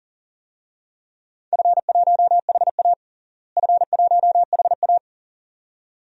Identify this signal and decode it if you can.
Morse code, heard as a keyed tone, about 40 wpm, 700 Hz. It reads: F1HA F1HA
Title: DXCC-100 Call Signs - 4 Characters - Repeated 2x - 40wpm